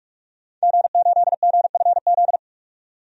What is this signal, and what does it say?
Morse code, heard as a keyed tone, about 42 wpm, 700 Hz. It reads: G8GFZ
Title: DXCC-100 Call Signs - 5 Characters - 45wpm